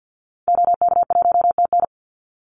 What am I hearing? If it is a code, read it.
Morse code, heard as a keyed tone, about 52 wpm, 700 Hz. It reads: OK1TN
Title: Call Signs Repeated 3x - 50wpm